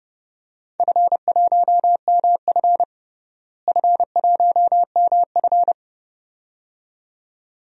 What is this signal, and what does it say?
Morse code, heard as a keyed tone, about 30 wpm, 700 Hz. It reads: F1MF F1MF
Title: DXCC-100 Call Signs - 4 Characters - Repeated 2x - 30wpm